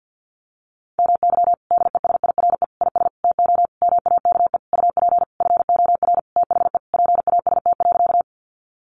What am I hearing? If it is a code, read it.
Morse code, heard as a keyed tone, about 50 wpm, 700 Hz. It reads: MY DESIRE IS TO MAKE UP FOR THE PAST.
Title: Sentences from Top 1100 Words 50wpm